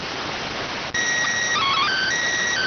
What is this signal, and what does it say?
SSTV (slow-scan television), PD 290.